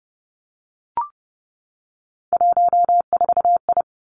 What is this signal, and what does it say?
Morse code, heard as a keyed tone, about 30 wpm, 700 Hz. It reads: 14I
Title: Three Challenging Letter-Number Combinations 30wpm